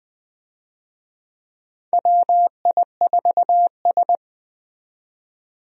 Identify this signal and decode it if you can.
Morse code, heard as a keyed tone, about 20 wpm, 700 Hz. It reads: WI4S
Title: USA Call Signs - 4 Characters - 20wpm